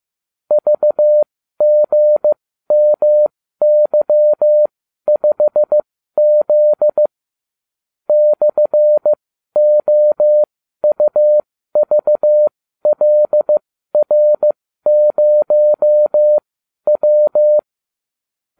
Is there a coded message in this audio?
Yes: Morse code from 0.5 to 17.6 s, heard as a keyed tone, about 15 wpm, spaced as for 12 wpm, 600 Hz. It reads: VGMY5Z /OUVLR0W